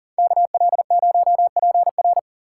Morse code, heard as a keyed tone, about 50 wpm, 700 Hz. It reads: KL0PR
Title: Speed Racing - USA 2x2 Call Signs 40wpm